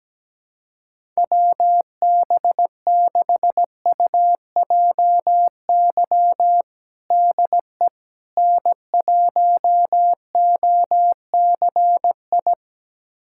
Morse code, heard as a keyed tone, about 17 wpm, 700 Hz. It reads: WB6UJY DE N1OCI
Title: USA Callsign DE Callsign 17wpm